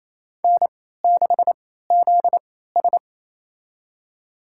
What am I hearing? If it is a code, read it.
Morse code, heard as a keyed tone, about 28 wpm, spaced as for 16 wpm, 700 Hz. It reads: N6ZS
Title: USA Call Signs - 4 Characters - 3x Character Spacing - 28wpm